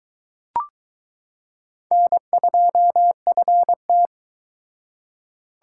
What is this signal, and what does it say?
Morse code, heard as a keyed tone, about 20 wpm, 700 Hz. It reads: N2FT